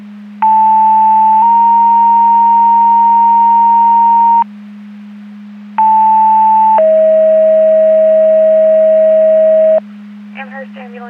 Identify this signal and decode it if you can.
Two-tone page, heard as a one-second tone, then a three-second tone, about 880 then 930 Hz, about 880 then 640 Hz, with speech. It is heard from 0.4 to 9.8 s.